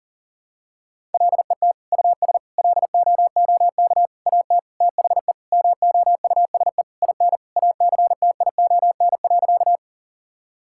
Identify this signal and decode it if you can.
Morse code, heard as a keyed tone, about 40 wpm, 700 Hz. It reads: LET US LOOK AT THE MOUSE IN ACTION.
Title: Sentences from Top 1000 Words - Review Part 5 - 40wpm